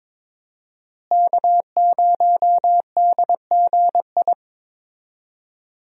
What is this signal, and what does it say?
Morse code, heard as a keyed tone, about 22 wpm, 700 Hz. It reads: K0DGI
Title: USA Call Signs - 5 Characters - Repeated 2x - 22wpm